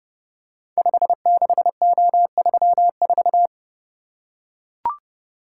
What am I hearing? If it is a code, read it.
Morse code, heard as a keyed tone, about 30 wpm, 700 Hz. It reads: 56O34
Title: Code Groups - 5 Letter:Numbers - 30wpm